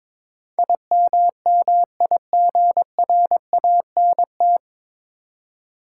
Morse code, heard as a keyed tone, about 21 wpm, 700 Hz. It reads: IMMIGRANT
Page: Morse Code Ninja practice 22wpm